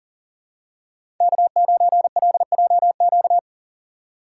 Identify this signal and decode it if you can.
Morse code, heard as a keyed tone, about 39 wpm, 700 Hz. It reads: K9LJQ
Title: USA Call Signs - 5 Characters - 6x Character Spacing - 40wpm